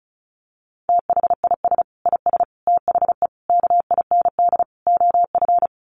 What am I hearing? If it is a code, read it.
Morse code, heard as a keyed tone, about 34 wpm, 700 Hz. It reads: THIS IS THE KIND OF
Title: Sets of 5 Words Encore 35wpm